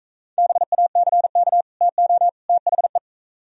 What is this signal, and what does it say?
Morse code, heard as a keyed tone, about 44 wpm, 700 Hz. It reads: BACK TO THE